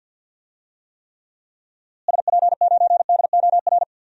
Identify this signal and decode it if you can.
Morse code, heard as a keyed tone, about 52 wpm, 700 Hz. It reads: SP9DOR
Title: DXCC-100 Call Signs - 6 Characters - Repeated 3x - 50wpm